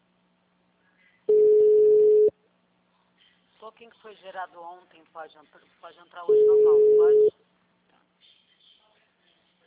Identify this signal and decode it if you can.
Ringback tone (Europe, 425 Hz), with speech.
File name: out- -2015- - - .8